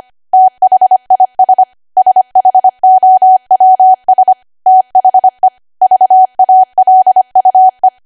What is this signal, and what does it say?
Morse code, heard as a keyed tone, about 25 wpm, 750 Hz. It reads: THIS SHOWS THE VALUE
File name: cwa / W1AW-Level3 / 25WPM